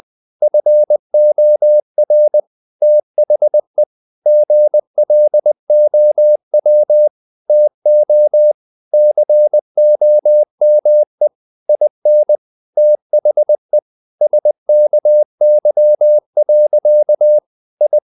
Morse code, heard as a keyed tone, about 20 wpm, 600 Hz. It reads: FOR THE GLOW TO COME IN THE SKY. I
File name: the old man and the sea part 2 20wpm